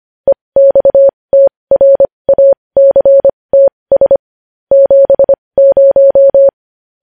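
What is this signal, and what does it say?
Morse code, heard as a keyed tone, about 25 wpm, spaced as for 20 wpm, 550 Hz. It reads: EXTRACTS 70
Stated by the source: Morse news headlines practice at 20 wpm.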